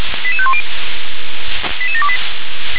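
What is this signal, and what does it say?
Five-tone selective call (ZVEI), heard as quick runs of five tones, tones of about 70 ms. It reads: 88510 88728
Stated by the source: ZVEI five-tone selective call received on SDR.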